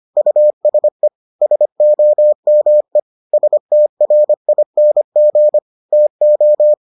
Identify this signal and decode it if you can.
Morse code, heard as a keyed tone, about 25 wpm, 600 Hz. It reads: USE SOME STRING TO